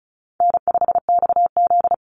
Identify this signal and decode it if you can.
Morse code, heard as a keyed tone, about 34 wpm, 700 Hz. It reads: N5XZ
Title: Call Signs Repeated 3x - 35wpm